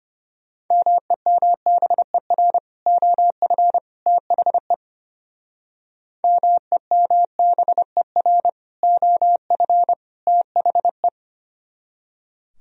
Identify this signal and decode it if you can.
Morse code, heard as a keyed tone, about 33 wpm, 700 Hz. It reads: MEMBER OF THE MEMBER OF THE